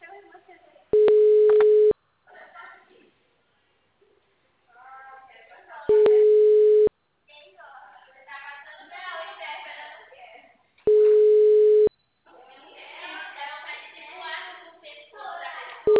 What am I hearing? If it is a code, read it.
Ringback tone (Europe, 425 Hz), with speech.